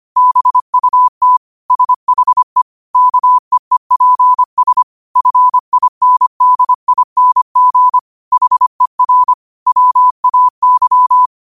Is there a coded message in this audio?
Yes: Morse code from 0.2 to 11.3 s, heard as a keyed tone, about 25 wpm, 1000 Hz. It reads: BUT SHE KEEPS FINDING HER WAY